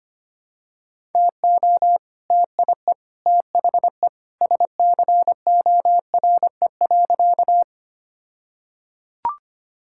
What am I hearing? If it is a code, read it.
Morse code, heard as a keyed tone, about 25 wpm, 700 Hz. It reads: TO TIE THE SCORE.
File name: practice / 25wpm / All-Age Appropriate Jokes - 25wpm